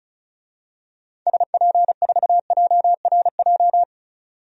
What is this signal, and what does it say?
Morse code, heard as a keyed tone, about 34 wpm, 700 Hz. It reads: SP4JRJ
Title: DXCC-100 Call Signs - 6 Characters - 35wpm